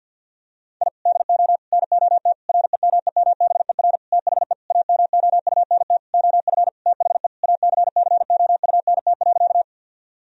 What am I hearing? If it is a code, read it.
Morse code, heard as a keyed tone, about 46 wpm, 700 Hz. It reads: I DO NOT REMEMBER THE AMOUNT OF THE ACCOUNT.
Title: Sentences from Top 1000 Words - Review Part 4 - 50wpm